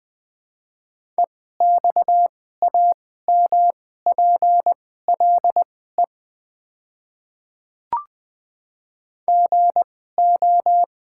Morse code, heard as a keyed tone, about 20 wpm, spaced as for 14 wpm, 700 Hz. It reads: EXAMPLE GO
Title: 2x Character Spacing - Top 300 Words - 20wpm